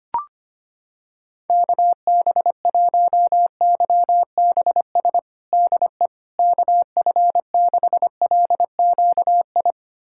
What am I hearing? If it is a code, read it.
Morse code, heard as a keyed tone, about 25 wpm, 700 Hz. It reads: KB1YBS DE KF6LQI
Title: USA Callsign DE Callsign - 6 Characters - 25wpm